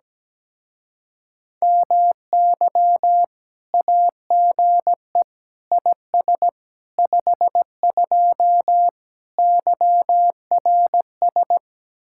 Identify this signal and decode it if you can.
Morse code, heard as a keyed tone, about 17 wpm, 700 Hz. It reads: MY AGE IS 52 YRS
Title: Speed-Racing Common QSO Phrases - 17wpm